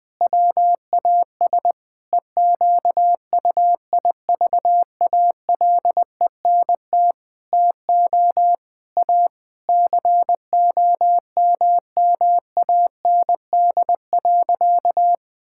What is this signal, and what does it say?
Morse code, heard as a keyed tone, about 20 wpm, 700 Hz. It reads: WAS EQUIVALENT TO A COMMAND.
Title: Sentences from Top 1600 Words 20wpm